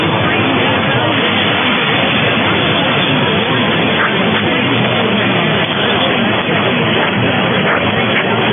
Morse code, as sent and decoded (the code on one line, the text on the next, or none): none